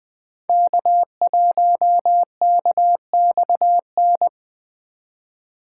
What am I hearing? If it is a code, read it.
Morse code, heard as a keyed tone, about 21 wpm, 700 Hz. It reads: K1KXN